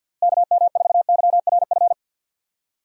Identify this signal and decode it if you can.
Morse code, heard as a keyed tone, about 49 wpm, 700 Hz. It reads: KM4YRF
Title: USA Call Signs - 6 Characters - Repeated 3x - 50wpm